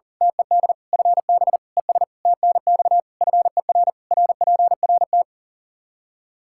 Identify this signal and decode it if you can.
Morse code, heard as a keyed tone, about 40 wpm, 700 Hz. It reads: TED FB ES TNX FER RPRT
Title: Practice QSO - Exchange 2 - Info - 40wpm